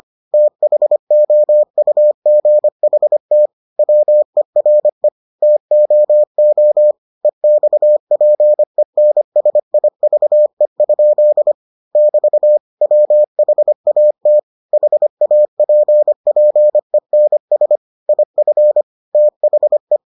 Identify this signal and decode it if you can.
Morse code, heard as a keyed tone, about 25 wpm, 600 Hz. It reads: THOUGHT WERE TOO EXPENSIVE? <BT> WHAT HAPPENS IF THE